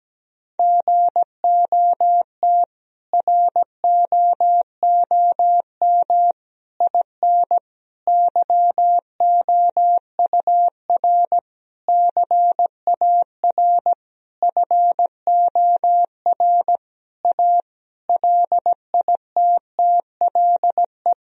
Morse code, heard as a keyed tone, about 17 wpm, 700 Hz. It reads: GOT ROOM IN YOUR CAR FOR A LITTLE
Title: Sentences from Top 1000 Words - Review Part 1 - 17wpm